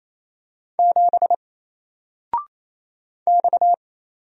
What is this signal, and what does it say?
Morse code, heard as a keyed tone, about 27 wpm, 700 Hz. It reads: 7 X
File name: Single-Letter-Number-28wpm